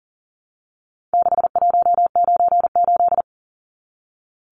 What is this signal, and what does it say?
Morse code, heard as a keyed tone, about 40 wpm, 700 Hz. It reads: 6198
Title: Numbers - 4-Digit 40wpm